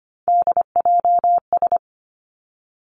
Morse code, heard as a keyed tone, about 25 wpm, 700 Hz. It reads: DJS
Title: Three Challenging Letter-Number Combinations 25wpm